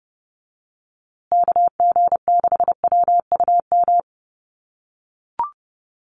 Morse code, heard as a keyed tone, about 30 wpm, 700 Hz. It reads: KG6WUM
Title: USA Call Signs Repeated 3x 30wpm